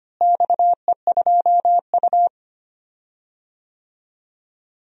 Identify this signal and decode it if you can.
Morse code, heard as a keyed tone, about 25 wpm, 700 Hz. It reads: XE2U
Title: DXCC-100 Call Signs - 4 Characters - Repeated 3x - 25wpm